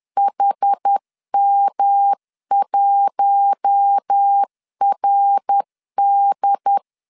Morse code, heard as a keyed tone, about 10 wpm, 800 Hz. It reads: HM1RD
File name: LEZIONI / 7 GF9Z / 7 G 10wpm 2501